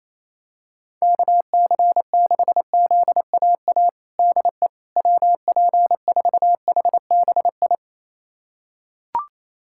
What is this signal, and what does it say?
Morse code, heard as a keyed tone, about 27 wpm, 700 Hz. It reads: KC6ZAA DE WP4HBI
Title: USA Callsign DE Callsign - 6 Characters - 28wpm